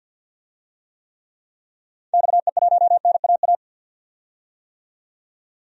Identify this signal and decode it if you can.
Morse code, heard as a keyed tone, about 52 wpm, 700 Hz. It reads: XE1NAA
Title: DXCC-100 Call Signs - 6 Characters - 50wpm